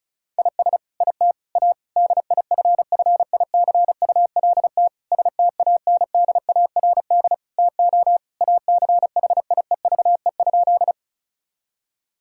Morse code, heard as a keyed tone, about 36 wpm, 700 Hz. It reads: IS IT A DIFFICULT STANDARD TO ACHIEVE?
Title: Sentences from Top 1000 Words - Review Part 6 - 35wpm